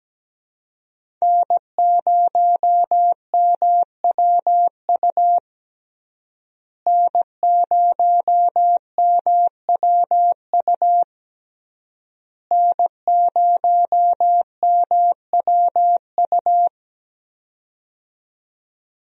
Morse code, heard as a keyed tone, about 17 wpm, 700 Hz. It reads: N0MWU N0MWU N0MWU
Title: USA Call Signs - 5 Characters - Repeated 3x - 17wpm